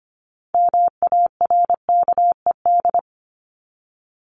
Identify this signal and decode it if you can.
Morse code, heard as a keyed tone, about 25 wpm, 700 Hz. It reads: MARKED
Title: ED Suffix - Top 100 Words with Base Words 25wpm